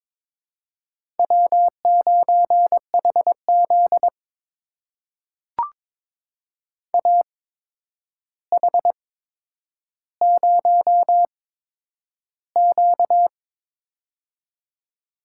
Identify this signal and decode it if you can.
Morse code, heard as a keyed tone, about 22 wpm, spaced as for 9 wpm, 700 Hz. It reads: W9HZ AH0Q
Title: USA Call Signs - 4 Characters - 8x Character Spacing - 22wpm